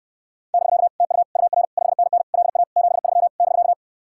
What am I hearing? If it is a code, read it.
Morse code, heard as a keyed tone, about 68 wpm, 700 Hz. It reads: <BK> TU UR 5NN LA 73 <BK>